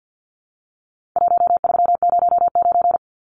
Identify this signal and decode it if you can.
Morse code, heard as a keyed tone, about 50 wpm, 700 Hz. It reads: 1309
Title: Numbers - 4-Digit 50wpm